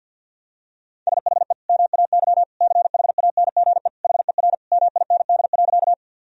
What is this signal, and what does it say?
Morse code, heard as a keyed tone, about 50 wpm, 700 Hz. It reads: SHE MAY CHANGE HER MIND.
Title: Sentences from Top 1000 Words - Review Part 1 - 50wpm